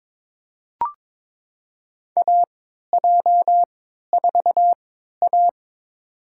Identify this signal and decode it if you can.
Morse code, heard as a keyed tone, about 22 wpm, spaced as for 13 wpm, 700 Hz. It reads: AJ4A
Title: USA Call Signs - 4 Characters - 3x Character Spacing - 22wpm